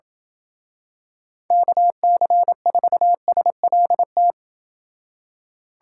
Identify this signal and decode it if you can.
Morse code, heard as a keyed tone, about 28 wpm, 700 Hz. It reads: KC4SLT